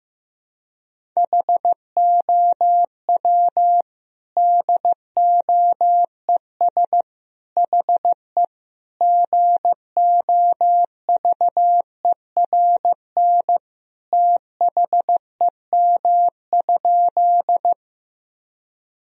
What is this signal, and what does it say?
Morse code, heard as a keyed tone, about 16 wpm, 700 Hz. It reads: HOW DOES HE GOVERN THEM?
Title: Sentences from Top 1000 Words - Review Part 2 - 15wpm